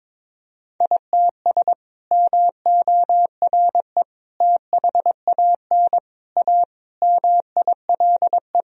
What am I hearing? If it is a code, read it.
Morse code, heard as a keyed tone, about 22 wpm, 700 Hz. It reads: ITS MORE THAN A MILE
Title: Sentences from Top 1000 Words - Review Part 1 - 22wpm